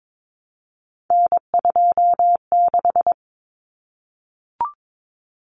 Morse code, heard as a keyed tone, about 22 wpm, 700 Hz. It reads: N26
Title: Common Cut Numbers - 3-Digit 22wpm